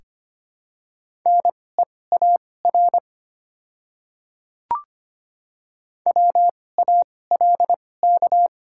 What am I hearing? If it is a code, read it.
Morse code, heard as a keyed tone, about 25 wpm, spaced as for 18 wpm, 700 Hz. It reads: NEAR WALK
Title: 2x Character Spacing - Top 300 Words - 25wpm